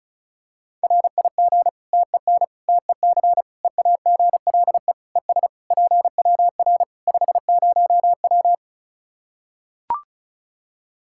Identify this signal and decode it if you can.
Morse code, heard as a keyed tone, about 34 wpm, 700 Hz. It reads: RIG TEN TEC EAGLE ES PWR 50W
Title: Practice QSO - Exchange 2 - Info - 35wpm